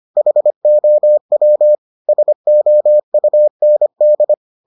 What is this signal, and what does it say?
Morse code, heard as a keyed tone, about 25 wpm, 600 Hz. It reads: HOW SOUND